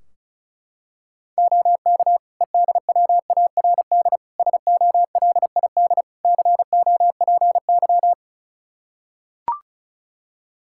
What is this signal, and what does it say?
Morse code, heard as a keyed tone, about 34 wpm, 700 Hz. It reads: OK EDWARD SOLID COPY
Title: Practice QSO - Exchange 3 - More Info - 35wpm